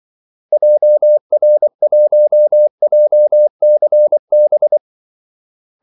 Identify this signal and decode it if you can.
Morse code, heard as a keyed tone, about 24 wpm, 600 Hz. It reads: JR1JCB
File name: Callsigns-24wpm